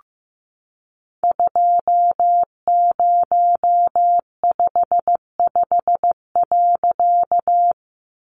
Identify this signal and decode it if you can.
Morse code, heard as a keyed tone, about 14 wpm, 700 Hz. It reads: 2055.
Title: A Sound Of Thunder 15wpm